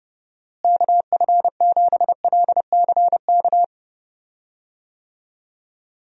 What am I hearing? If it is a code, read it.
Morse code, heard as a keyed tone, about 30 wpm, 700 Hz. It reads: KF7LCK
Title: USA Call Signs - 6 Characters - Repeated 3x - 30wpm